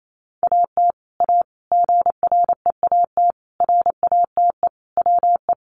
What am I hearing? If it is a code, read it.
Morse code, heard as a keyed tone, about 28 wpm, 700 Hz. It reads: AT A GREAT RATE WE
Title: Sentences from Top 1200 Words 28wpm